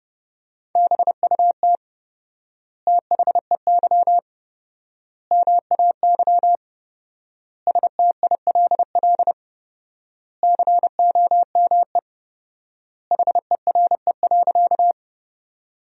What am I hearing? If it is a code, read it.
Morse code, heard as a keyed tone, about 30 wpm, 700 Hz. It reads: BUT THEY MAY STILL COME HERE.
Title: 4x Word-Spacing - Sentences from Top 300 Words - 30wpm